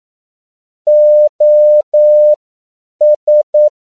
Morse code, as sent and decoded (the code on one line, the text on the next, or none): --- ...
OS